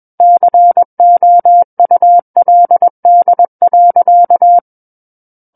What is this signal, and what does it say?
Morse code, heard as a keyed tone, about 21 wpm, 700 Hz. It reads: COULD.
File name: Book One / 12-Chapter 12 - 21 WPM